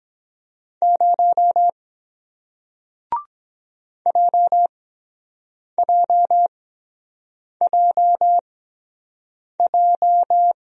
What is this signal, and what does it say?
Morse code, heard as a keyed tone, about 23 wpm, spaced as for 7 wpm, 700 Hz. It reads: 0 JJJJ